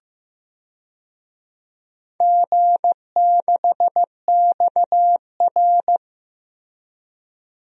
Morse code, heard as a keyed tone, about 15 wpm, 700 Hz. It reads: G6XR
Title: DXCC-100 Call Signs - 4 Characters - Repeated 2x - 15wpm